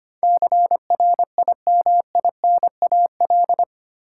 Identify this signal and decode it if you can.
Morse code, heard as a keyed tone, about 25 wpm, 700 Hz. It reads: CRIMINAL
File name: 8 Letter Words - First 3 Letters Given - 25wpm